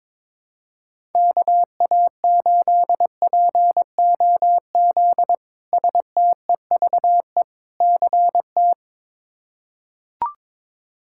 Morse code, heard as a keyed tone, about 22 wpm, 700 Hz. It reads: KA8POZ STEVE CT
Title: K1USN SST QSO Practice - 22wpm